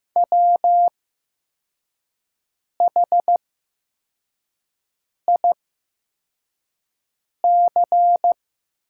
Morse code, heard as a keyed tone, about 15 wpm, spaced as for 4 wpm, 700 Hz. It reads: WHIC
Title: 8x Character Spacing - Top 200 Words - 15wpm